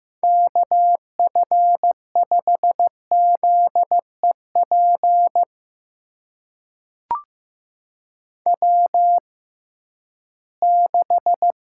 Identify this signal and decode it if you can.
Morse code, heard as a keyed tone, about 15 wpm, spaced as for 11 wpm, 700 Hz. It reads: KF5ZEP W6
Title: USA Call Signs - 6x Character Spacing - 15wpm